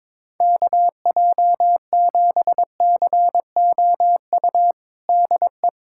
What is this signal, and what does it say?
Morse code, heard as a keyed tone, about 22 wpm, 700 Hz. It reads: KJ7COU DE
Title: USA Callsign DE Callsign 22wpm